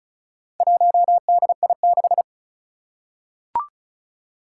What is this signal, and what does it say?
Morse code, heard as a keyed tone, about 36 wpm, 700 Hz. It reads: 1DI6
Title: Four Challenging Letter-Number Combinations 35wpm